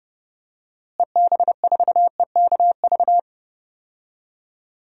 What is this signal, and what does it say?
Morse code, heard as a keyed tone, about 30 wpm, 700 Hz. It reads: EB4EKV
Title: DXCC-100 Call Signs - 6 Characters - Repeated 3x - 30wpm